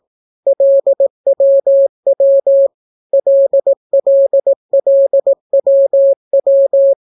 Morse code, heard as a keyed tone, about 18 wpm, 550 Hz. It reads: LWW LLLWW